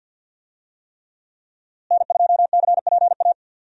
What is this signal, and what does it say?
Morse code, heard as a keyed tone, about 52 wpm, 700 Hz. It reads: N2CPA
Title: USA Call Signs - 5 Characters - Repeated 3x - 50wpm